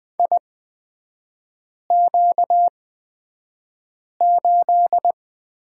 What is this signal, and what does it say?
Morse code, heard as a keyed tone, about 20 wpm, spaced as for 5 wpm, 690 Hz. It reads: IQ8
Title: 255 Lesson 41e - Period - Three Characters 20wpm